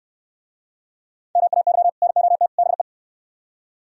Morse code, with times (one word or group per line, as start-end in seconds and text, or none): 1.4-1.9 MAY
2.0-2.5 NOT
2.6-2.8 BE